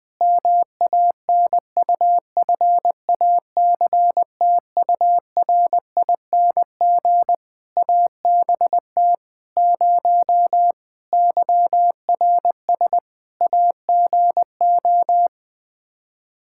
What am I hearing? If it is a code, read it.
Morse code, heard as a keyed tone, about 20 wpm, 700 Hz. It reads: MANUFACTURING ABT 0 YRS AGO
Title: Practice QSO - Exchange 3 - More Info - 20wpm